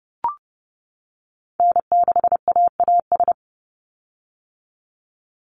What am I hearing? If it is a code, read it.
Morse code, heard as a keyed tone, about 30 wpm, 700 Hz. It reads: N6AAS
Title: USA Call Signs 30wpm